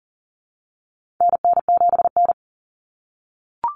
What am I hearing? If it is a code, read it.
Morse code, heard as a keyed tone, about 40 wpm, 700 Hz. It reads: NN7N
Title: Common Cut Numbers - 4-Digit 40wpm